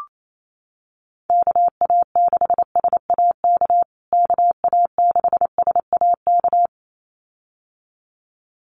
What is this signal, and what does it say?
Morse code, heard as a keyed tone, about 28 wpm, 700 Hz. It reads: KA6SAK KA6SAK
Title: USA Call Signs Repeated 2x 28wpm